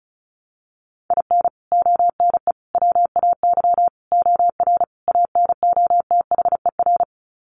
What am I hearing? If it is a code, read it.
Morse code, heard as a keyed tone, about 34 wpm, 700 Hz. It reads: IN ONE WAY OR ANOTHER
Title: Sets of 5 Words Encore 35wpm